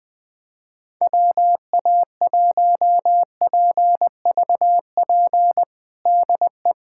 Morse code, heard as a keyed tone, about 20 wpm, 700 Hz. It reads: WA1PVP DE
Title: USA Callsign DE Callsign - 6 Characters - 20wpm